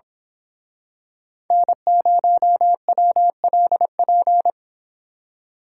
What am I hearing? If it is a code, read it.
Morse code, heard as a keyed tone, about 27 wpm, 700 Hz. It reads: N0WLP
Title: Speed Racing - USA 1x3 Call Signs 22wpm